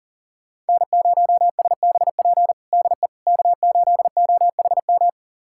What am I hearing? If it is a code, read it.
Morse code, heard as a keyed tone, about 40 wpm, 700 Hz. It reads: N0SBP DE K8OHM
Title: USA Callsign DE Callsign - 5 Characters - 40wpm